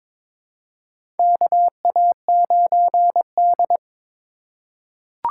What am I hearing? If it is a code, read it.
Morse code, heard as a keyed tone, about 21 wpm, 700 Hz. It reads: KA9D
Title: USA Call Signs - 4 Characters - 2x Character Spacing - 22wpm